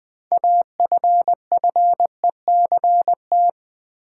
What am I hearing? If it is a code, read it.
Morse code, heard as a keyed tone, about 20 wpm, 700 Hz. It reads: AFFECT
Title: Six Letter Words - 20wpm